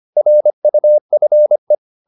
Morse code, heard as a keyed tone, about 25 wpm, 600 Hz. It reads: RUFE